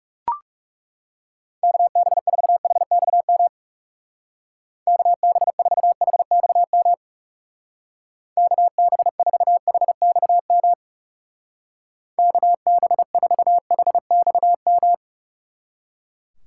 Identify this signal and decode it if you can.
Morse code, heard as a keyed tone, about 36 wpm, 700 Hz. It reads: KB4HXM KB4HXM KB4HXM KB4HXM
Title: Speed Racing - USA 2x3 Call Signs 30wpm